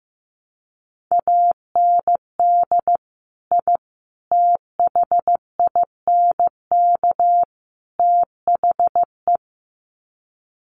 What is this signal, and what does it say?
Morse code, heard as a keyed tone, about 15 wpm, 700 Hz. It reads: AND I THINK THE
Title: Sets of 4 Words 15wpm